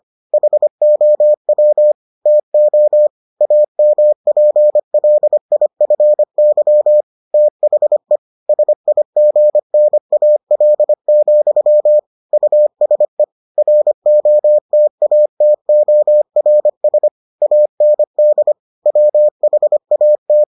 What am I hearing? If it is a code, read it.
Morse code, heard as a keyed tone, about 25 wpm, 600 Hz. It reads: HOW TO AMPLIFY THE SIGNAL, USE ROTATORS AND WHAT